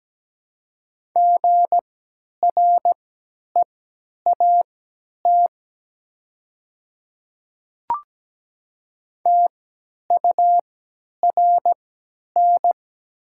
Morse code, heard as a keyed tone, about 17 wpm, spaced as for 10 wpm, 690 Hz. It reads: GREAT TURN
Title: 3x Character Spacing - Top 300 Words - 17wpm